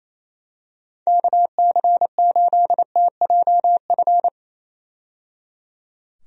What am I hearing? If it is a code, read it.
Morse code, heard as a keyed tone, about 27 wpm, 700 Hz. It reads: KC8TJF
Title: USA Call Signs - 6 Characters - 28wpm